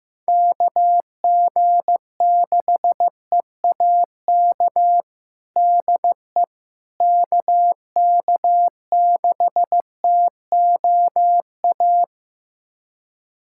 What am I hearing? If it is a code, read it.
Morse code, heard as a keyed tone, about 15 wpm, 700 Hz. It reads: KG6EAK DE KK6TOA
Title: USA Callsign DE Callsign - 6 Characters - 15wpm